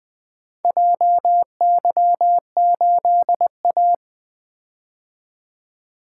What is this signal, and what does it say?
Morse code, heard as a keyed tone, about 25 wpm, 700 Hz. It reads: JY8A
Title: Speed-Racing - Four Letter-Number Combinations - 20wpm